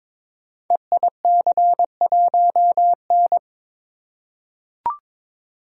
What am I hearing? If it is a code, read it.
Morse code, heard as a keyed tone, about 22 wpm, 700 Hz. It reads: EIC1N
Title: Code Groups - 5 Letter:Numbers - 22wpm